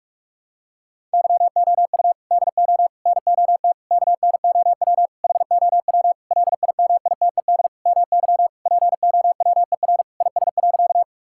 Morse code, heard as a keyed tone, about 41 wpm, 700 Hz. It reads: YOU DO NOT KNOW HOW LIMITED MY POWER IS.
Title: Sentences from Top 1000 Words - Review Part 6 - 45wpm